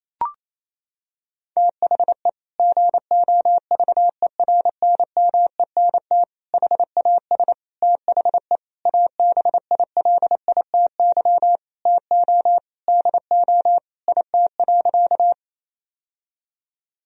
Morse code, heard as a keyed tone, about 28 wpm, 700 Hz. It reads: THE GOVERNMENT HAS THE ABILITY TO DO IT.
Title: Sentences from Top 1000 Words - Review Part 3 - 28wpm